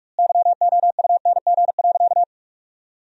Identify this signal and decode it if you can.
Morse code, heard as a keyed tone, about 46 wpm, 700 Hz. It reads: YOUNG.
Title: 2x Word-Spacing - Sentences from Top 400 Words - 45wpm